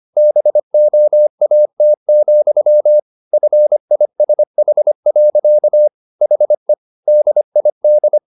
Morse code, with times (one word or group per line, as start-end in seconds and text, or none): 0.2-3.0 BOAT,
3.3-5.9 FISH.
6.2-6.7 HE
7.1-8.2 DID